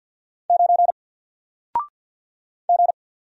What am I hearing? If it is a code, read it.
Morse code, heard as a keyed tone, about 50 wpm, 700 Hz. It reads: 9 G
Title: Single Character (Letter:Number:Punctuation:Prosign) 50wpm